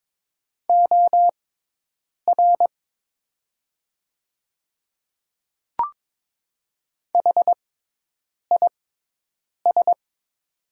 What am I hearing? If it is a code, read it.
Morse code, heard as a keyed tone, about 22 wpm, spaced as for 8 wpm, 700 Hz. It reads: OR HIS
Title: 6x Character Spacing - Top 100 Words - 22wpm